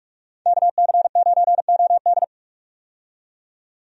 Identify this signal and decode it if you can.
Morse code, heard as a keyed tone, about 46 wpm, 700 Hz. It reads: KC9OD
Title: USA Call Signs - 5 Characters - 45wpm